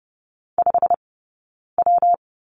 Morse code, heard as a keyed tone, about 30 wpm, 700 Hz. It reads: 5 W
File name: Lesson 19d - Letter w - Two Characters 30wpm